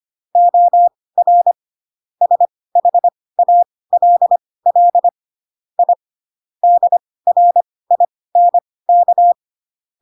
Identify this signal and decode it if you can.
Morse code, heard as a keyed tone, about 25 wpm, spaced as for 18 wpm, 700 Hz. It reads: OR SHALL I DRINK